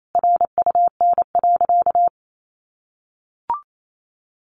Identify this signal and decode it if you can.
Morse code, heard as a keyed tone, about 28 wpm, 700 Hz. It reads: RUN.
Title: Sentences from Top 1500 Words 28wpm